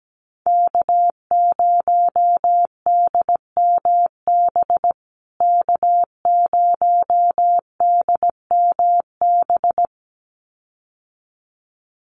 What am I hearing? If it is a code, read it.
Morse code, heard as a keyed tone, about 17 wpm, 700 Hz. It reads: K0DMB K0DMB
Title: USA Call Signs Repeated 2x 17wpm